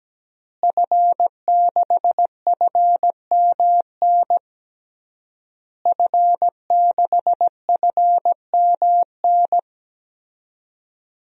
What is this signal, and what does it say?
Morse code, heard as a keyed tone, about 17 wpm, 700 Hz. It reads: F6FMN F6FMN
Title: DXCC-100 Call Signs - 5 Characters - Repeated 3x - 17wpm